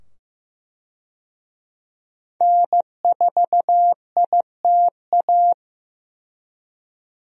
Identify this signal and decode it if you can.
Morse code, heard as a keyed tone, about 15 wpm, 700 Hz. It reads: N4ITA
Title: USA Call Signs - 5 Characters - 15wpm